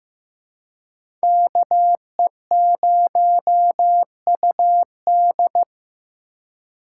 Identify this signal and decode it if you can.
Morse code, heard as a keyed tone, about 18 wpm, 700 Hz. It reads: KE0UD
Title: Speed Racing - USA 2x2 Call Signs 15wpm